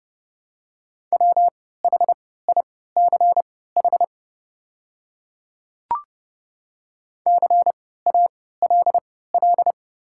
Morse code, heard as a keyed tone, about 30 wpm, spaced as for 17 wpm, 700 Hz. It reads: WHICH CALL
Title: 3x Character Spacing - Top 200 Words - 30wpm